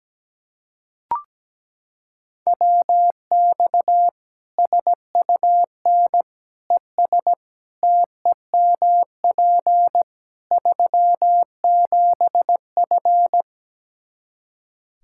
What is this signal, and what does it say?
Morse code, heard as a keyed tone, about 17 wpm, 700 Hz. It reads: WX SUN ES TEMP 37F
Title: Practice QSO - Exchange 2 - Info - 17wpm